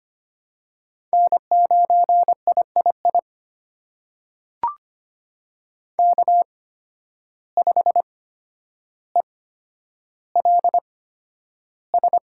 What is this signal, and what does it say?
Morse code, heard as a keyed tone, about 25 wpm, spaced as for 11 wpm, 700 Hz. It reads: N9III K5ELS
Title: USA Call Signs - 5 Characters - 8x Character Spacing - 25wpm